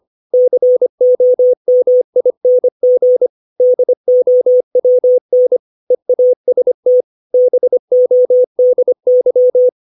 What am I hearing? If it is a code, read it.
Morse code, heard as a keyed tone, about 25 wpm, 500 Hz. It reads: COMING DOWN EAST BODY
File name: 632 words / 632 Words 25WPM